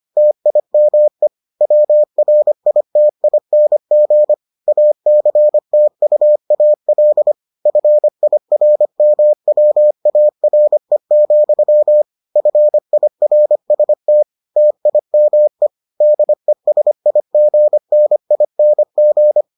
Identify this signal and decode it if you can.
Morse code, heard as a keyed tone, about 25 wpm, 600 Hz. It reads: TIME WRITING ACTUAL FIRMWARE, FIRST TIME DESIGNING